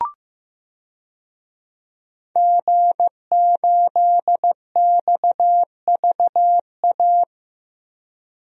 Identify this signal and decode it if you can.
Morse code, heard as a keyed tone, about 15 wpm, 700 Hz. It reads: G8XVA
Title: DXCC-100 Call Signs - 5 Characters - Repeated 2x - 15wpm